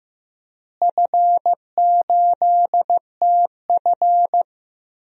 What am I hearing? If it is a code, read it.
Morse code, heard as a keyed tone, about 15 wpm, 700 Hz. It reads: F8TF
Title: DXCC-100 Call Signs - 4 Characters - Repeated 2x - 15wpm